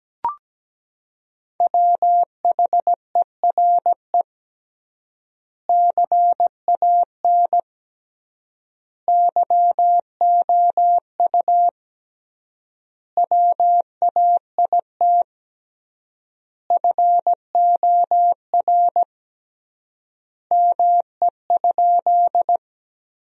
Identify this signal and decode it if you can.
Morse code, heard as a keyed tone, about 17 wpm, 700 Hz. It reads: WHERE CAN YOU WAIT FOR ME?
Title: 3x Word-Spacing - Sentences from Top 500 Words - 17wpm